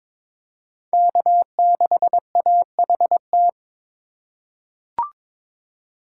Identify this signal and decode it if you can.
Morse code, heard as a keyed tone, about 22 wpm, 700 Hz. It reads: K6AHT
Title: USA Call Signs - 5 Characters - 2x Character Spacing - 22wpm